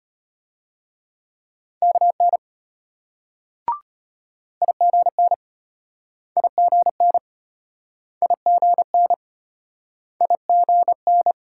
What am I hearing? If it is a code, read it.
Morse code, heard as a keyed tone, about 33 wpm, 700 Hz. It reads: KN IGN IGN IGN IGN